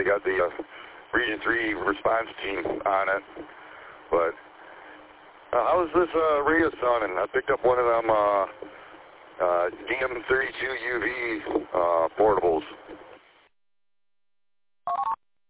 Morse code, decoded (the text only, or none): none